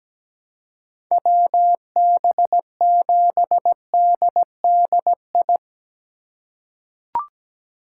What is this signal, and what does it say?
Morse code, heard as a keyed tone, about 17 wpm, 700 Hz. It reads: WB7DDI
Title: USA Call Signs - 6 Characters - 3x Character Spacing - 17wpm